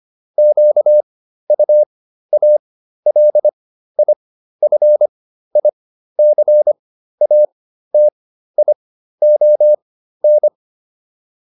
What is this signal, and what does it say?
Morse code, heard as a keyed tone, about 25 wpm, spaced as for 13 wpm, 600 Hz. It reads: QUALIFICATION